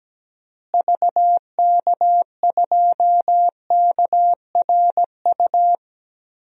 Morse code, heard as a keyed tone, about 17 wpm, 700 Hz. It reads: VK2KRU
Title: DXCC-100 Call Signs - 6 Characters - Repeated 2x - 17wpm